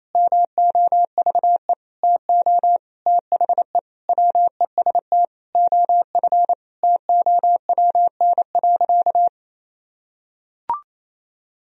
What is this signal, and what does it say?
Morse code, heard as a keyed tone, about 27 wpm, 700 Hz. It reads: MOVE TO THE WEST OF TOWN.
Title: Sentences from Top 1000 Words - Review Part 2 - 28wpm